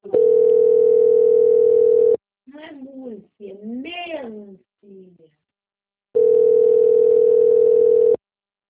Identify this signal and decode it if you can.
Ringback tone (North America), 440 + 480 Hz, with speech.
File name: out- - - - - .17239